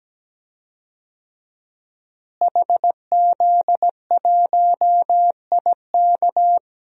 Morse code, heard as a keyed tone, about 17 wpm, 700 Hz. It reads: HZ1IK
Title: DXCC-100 Call Signs - 5 Characters - Repeated 3x - 17wpm